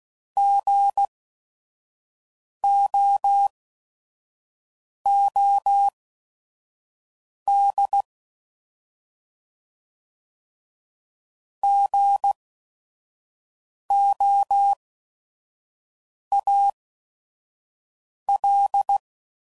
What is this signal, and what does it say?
Morse code, heard as a keyed tone, about 16 wpm, spaced as for 5 wpm, 800 Hz. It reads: GOOD GOAL
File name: Prefixes / By First Letter / Gs